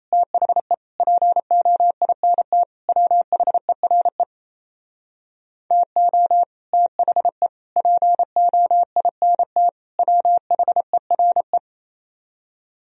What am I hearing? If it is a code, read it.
Morse code, heard as a keyed tone, about 36 wpm, 700 Hz. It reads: THE POINT WHERE TO THE POINT WHERE